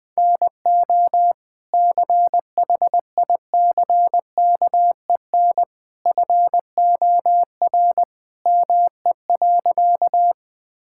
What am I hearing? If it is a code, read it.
Morse code, heard as a keyed tone, about 21 wpm, 700 Hz. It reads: NO CHICKEN FOR ME.
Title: Sentences from Top 1000 Words - Review Part 5 - 20wpm